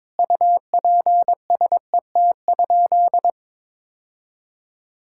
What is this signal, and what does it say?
Morse code, heard as a keyed tone, about 22 wpm, 700 Hz. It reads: UPSET?
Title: Sentences from Top 1000 Words - Review Part 5 - 22wpm